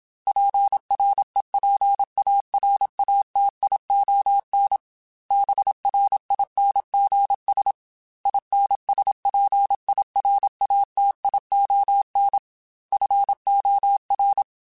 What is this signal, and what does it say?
Morse code, heard as a keyed tone, about 27 wpm, 800 Hz. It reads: PREPARATION BRINGS INSPIRATION FOR
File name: Quotes / Quotes-mp3s / Persperation @25